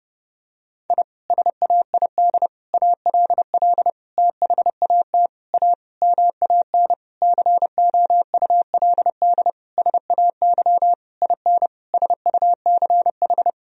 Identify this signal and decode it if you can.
Morse code, heard as a keyed tone, about 30 wpm, 700 Hz. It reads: I SAID ALL THAT A MAN COULD SAY IN SUCH
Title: Sentences from Top 1000 Words - Review Part 2 - 30wpm